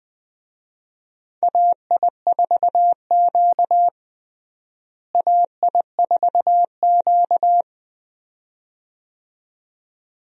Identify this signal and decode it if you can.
Morse code, heard as a keyed tone, about 20 wpm, 700 Hz. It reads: AI4Q AI4Q
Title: USA Call Signs - 4 Characters - Repeated 2x - 20wpm